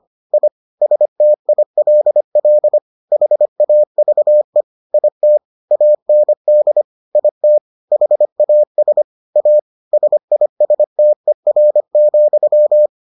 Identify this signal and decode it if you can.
Morse code, heard as a keyed tone, about 25 wpm, 600 Hz. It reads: I STILL HAVE IT AND IT HAS A SISTER,